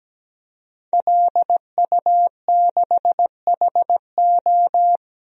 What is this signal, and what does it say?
Morse code, heard as a keyed tone, about 17 wpm, 700 Hz. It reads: LU6HO
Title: DXCC-100 Call Signs - 5 Characters - Repeated 3x - 17wpm